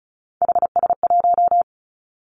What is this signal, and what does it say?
Morse code, heard as a keyed tone, about 36 wpm, 700 Hz. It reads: HS1
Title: Three Challenging Letter-Number Combinations 35wpm